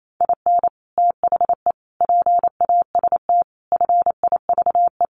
Morse code, heard as a keyed tone, about 28 wpm, 700 Hz. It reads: IN THE PAST FIVE
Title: Sets of 5 Words Encore 28wpm